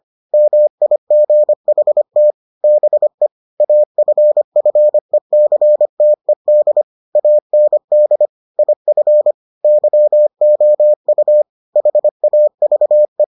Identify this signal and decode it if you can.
Morse code, heard as a keyed tone, about 25 wpm, 600 Hz. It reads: MIGHT BE AFFECTED AND IF YOU HAVE